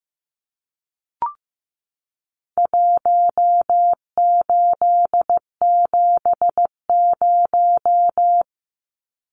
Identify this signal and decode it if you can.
Morse code, heard as a keyed tone, about 15 wpm, 700 Hz. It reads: 1870
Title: Numbers - Contest Style - 4-Digit 15wpm